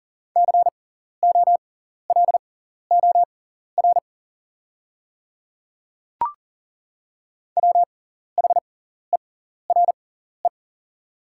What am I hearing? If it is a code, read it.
Morse code, heard as a keyed tone, about 40 wpm, spaced as for 14 wpm, 690 Hz. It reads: COLOR WHERE
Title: 6x Character Spacing - Top 500 Words - 40wpm